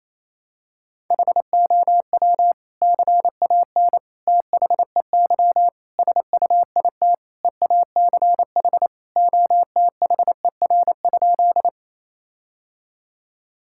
Morse code, heard as a keyed tone, about 28 wpm, 700 Hz. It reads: HOW CAN THEY SUIT EACH OTHER?
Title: Sentences from Top 1600 Words 28wpm